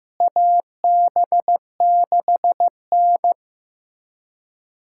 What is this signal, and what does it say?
Morse code, heard as a keyed tone, about 16 wpm, 700 Hz. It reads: AB6N
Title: USA Callsign DE Callsign - 4 Characters - 15wpm